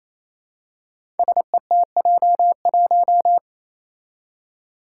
Morse code, heard as a keyed tone, about 27 wpm, 700 Hz. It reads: SETJ1
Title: Five Challenging Letter-Number Combinations 28wpm